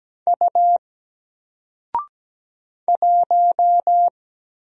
Morse code, heard as a keyed tone, about 17 wpm, 700 Hz. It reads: U 1
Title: Single Character (Letter:Number:Punctuation:Prosign) Alt 17wpm